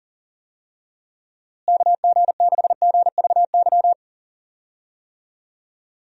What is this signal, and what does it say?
Morse code, heard as a keyed tone, about 40 wpm, 700 Hz. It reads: KG6GVY
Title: USA Call Signs - 6 Characters - Repeated 3x - 40wpm